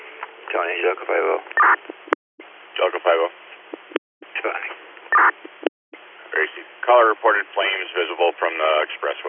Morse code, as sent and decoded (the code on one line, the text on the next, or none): none